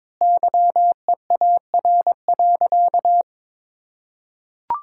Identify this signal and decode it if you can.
Morse code, heard as a keyed tone, about 22 wpm, 700 Hz. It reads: YEAR.
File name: practice / 22wpm / Sentences with 8-Word Phrases - 22wpm